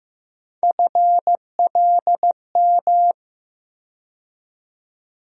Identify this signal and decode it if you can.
Morse code, heard as a keyed tone, about 15 wpm, 700 Hz. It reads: FLM
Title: USA Call Sign Postfixes - 3 Letters 15wpm